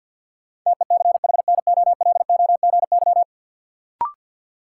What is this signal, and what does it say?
Morse code, heard as a keyed tone, about 53 wpm, 700 Hz. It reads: TECHNOLOGY
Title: Sentences from Top 1200 Words - New Word Drill Alt - 50wpm